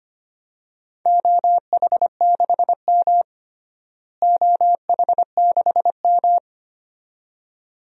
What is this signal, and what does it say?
Morse code, heard as a keyed tone, about 25 wpm, 700 Hz. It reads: OH6M OH6M
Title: DXCC-100 Call Signs - 4 Characters - Repeated 2x - 25wpm